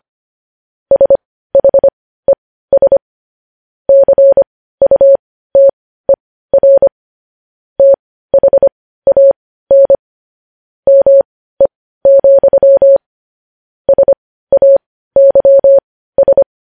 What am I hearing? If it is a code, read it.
Morse code, heard as a keyed tone, about 25 wpm, spaced as for 15 wpm, 550 Hz. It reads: SHES CUTER THAN ME, SAYS
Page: News Headlines in Morse Code at 15 WPM